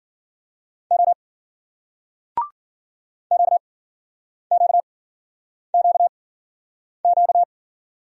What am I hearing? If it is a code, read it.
Morse code, heard as a keyed tone, about 50 wpm, spaced as for 9 wpm, 700 Hz. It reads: O QQQQ